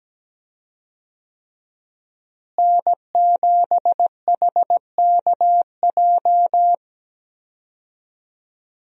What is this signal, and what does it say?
Morse code, heard as a keyed tone, about 17 wpm, 700 Hz. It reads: N7HKJ
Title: USA Call Signs - 5 Characters - Repeated 3x - 17wpm